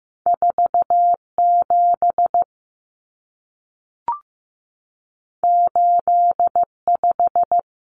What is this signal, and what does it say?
Morse code, heard as a keyed tone, about 15 wpm, 700 Hz. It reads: 47 85
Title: Numbers - 2-Digit 15wpm